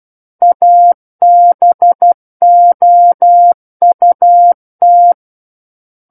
Morse code, heard as a keyed tone, about 12 wpm, 700 Hz. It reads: ABOUT
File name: Book One / 03-Chapter 3 - 12 WPM